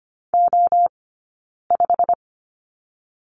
Morse code, heard as a keyed tone, about 24 wpm, 700 Hz. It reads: O 5
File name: Lesson 15d - Number 5 - Two Characters 25wpm